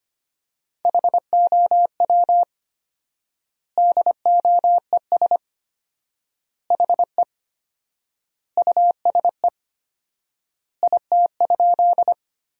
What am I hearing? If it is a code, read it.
Morse code, heard as a keyed tone, about 25 wpm, 700 Hz. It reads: HOW DOES HE USE IT?
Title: 4x Word-Spacing - Sentences from Top 200 Words - 25wpm